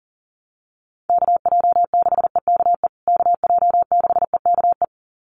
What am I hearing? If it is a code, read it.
Morse code, heard as a keyed tone, about 37 wpm, 700 Hz. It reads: KJ6EKE KJ6EKE
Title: USA Call Signs Repeated 2x 40wpm